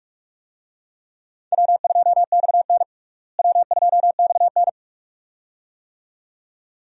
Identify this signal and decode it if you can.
Morse code, heard as a keyed tone, about 46 wpm, 700 Hz. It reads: W2XN W2XN
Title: USA Call Signs - 4 Characters - Repeated 2x - 45wpm